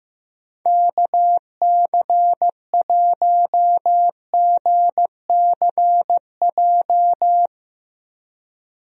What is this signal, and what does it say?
Morse code, heard as a keyed tone, about 15 wpm, 700 Hz. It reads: KC1GCJ
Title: USA Call Signs - 6 Characters - Repeated 3x - 15wpm